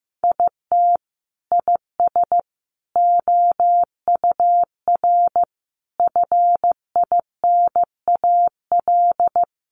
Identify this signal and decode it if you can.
Morse code, heard as a keyed tone, about 15 wpm, 700 Hz. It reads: IT IS OUR FINAL
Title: Sentences from Top 1200 Words 15wpm